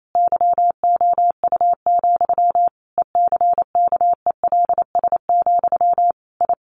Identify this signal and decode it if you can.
Morse code, heard as a keyed tone, about 28 wpm, 700 Hz. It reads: YOU, ECKELS, I
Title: A Sound Of Thunder 28wpm